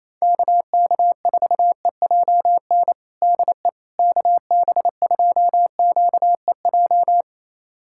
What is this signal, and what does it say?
Morse code, heard as a keyed tone, about 28 wpm, 700 Hz. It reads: KK4EJN DE KB2QEJ
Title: USA Callsign DE Callsign 28wpm